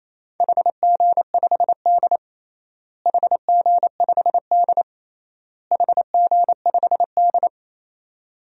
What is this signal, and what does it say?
Morse code, heard as a keyed tone, about 28 wpm, 700 Hz. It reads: HG5D HG5D HG5D
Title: DXCC-100 Call Signs - 4 Characters - Repeated 3x - 28wpm